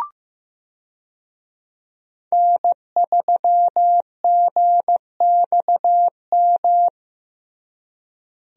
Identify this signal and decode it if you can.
Morse code, heard as a keyed tone, about 15 wpm, 700 Hz. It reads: N3GXM
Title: USA Call Signs - 5 Characters - Repeated 2x - 15wpm